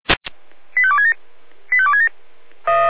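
Five-tone selective call (ZVEI), heard as quick runs of five tones, tones of about 70 ms. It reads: 85267 85267
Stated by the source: ZVEI five-tone selective call received on SDR.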